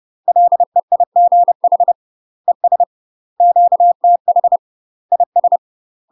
Morse code, heard as a keyed tone, about 30 wpm, 700 Hz. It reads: LEIGH ES QTH IS
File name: UK MASTER QSOs TRAINING / 30WPM / UK MASTER QSOs 0035